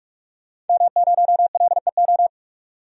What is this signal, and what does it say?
Morse code, heard as a keyed tone, about 46 wpm, 700 Hz. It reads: M0LEO
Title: DXCC-100 Call Signs - 5 Characters - 45wpm